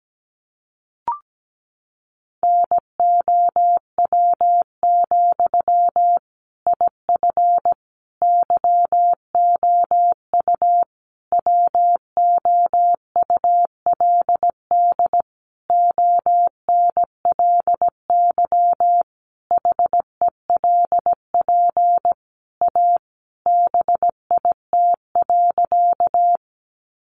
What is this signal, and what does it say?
Morse code, heard as a keyed tone, about 17 wpm, 700 Hz. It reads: NOW, IF YOU WOULD ONLY HELP A BIT.
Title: Sentences from Top 900 Words 17wpm